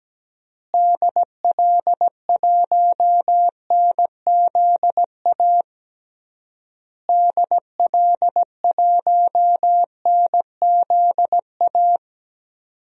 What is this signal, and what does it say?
Morse code, heard as a keyed tone, about 17 wpm, 700 Hz. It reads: DL1NZA DL1NZA
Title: DXCC-100 Call Signs - 6 Characters - Repeated 3x - 17wpm